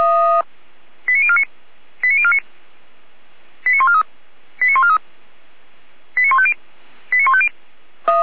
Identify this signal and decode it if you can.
Five-tone selective call (ZVEI), heard as quick runs of five tones, tones of about 70 ms. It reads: 89049 89049 89143 89143 89150 89150
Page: ZVEI